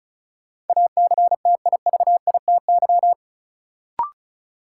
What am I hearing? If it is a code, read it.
Morse code, heard as a keyed tone, about 36 wpm, 700 Hz. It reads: ACTIVITY
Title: Sentences from Top 1000 Words - Review Part 4 - New Word Drill - 35wpm